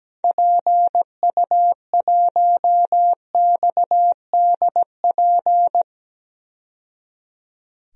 Morse code, heard as a keyed tone, about 17 wpm, 700 Hz. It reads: PU1XDP
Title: DXCC-100 Call Signs - 6 Characters - 17wpm